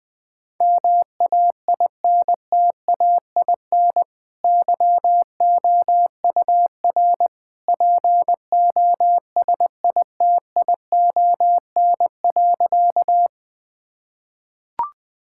Morse code, heard as a keyed tone, about 20 wpm, 700 Hz. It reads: MAINTAIN YOUR POSITION.
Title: Sentences from Top 1000 Words - Review Part 5 - 20wpm